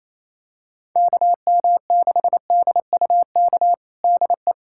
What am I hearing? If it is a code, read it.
Morse code, heard as a keyed tone, about 28 wpm, 700 Hz. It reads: KM6DUK DE
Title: USA Callsign DE Callsign - 6 Characters - 28wpm